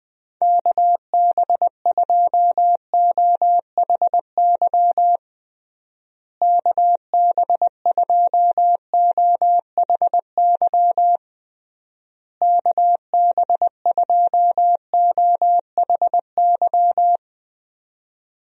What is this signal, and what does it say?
Morse code, heard as a keyed tone, about 20 wpm, 700 Hz. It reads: KB2OHY KB2OHY KB2OHY
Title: USA Call Signs - 6 Characters - Repeated 3x - 20wpm